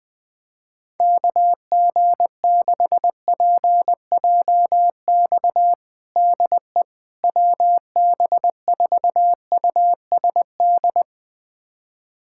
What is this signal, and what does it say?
Morse code, heard as a keyed tone, about 20 wpm, 700 Hz. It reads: KG6PJX DE WB4USD
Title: USA Callsign DE Callsign - 6 Characters - 20wpm